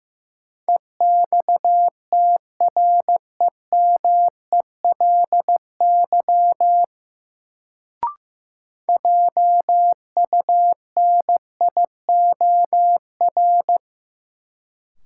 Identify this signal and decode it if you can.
Morse code, heard as a keyed tone, about 15 wpm, 700 Hz. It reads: EXTREMELY JUNIOR
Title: Sentences from Top 1000 Words - Review Part 7 - New Word Drill - 15wpm